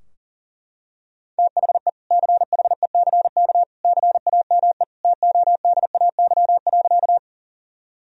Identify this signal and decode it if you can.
Morse code, heard as a keyed tone, about 40 wpm, 700 Hz. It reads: THE CHECK CAME TODAY.
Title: Sentences from Top 1000 Words - Review Part 3 - 40wpm